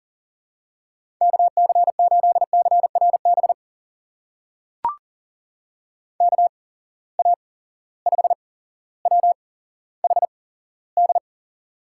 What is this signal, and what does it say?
Morse code, heard as a keyed tone, about 40 wpm, spaced as for 17 wpm, 700 Hz. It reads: KC8CRB KA5WHD
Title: USA Call Signs - 6 Characters - 8x Character Spacing - 40wpm